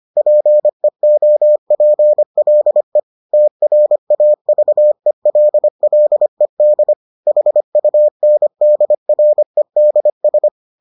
Morse code, heard as a keyed tone, about 25 wpm, 600 Hz. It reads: PEOPLE TRAVELLED HUNDREDS